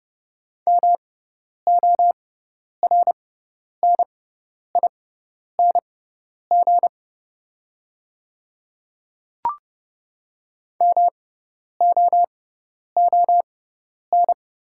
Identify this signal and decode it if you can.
Morse code, heard as a keyed tone, about 30 wpm, spaced as for 10 wpm, 690 Hz. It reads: MORNING MOON
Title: 6x Character Spacing - Top 500 Words - 30wpm